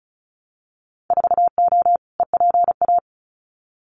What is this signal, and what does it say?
Morse code, heard as a keyed tone, about 36 wpm, 690 Hz. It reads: 4O EPA
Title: Winter Field Day Exchanges 35wpm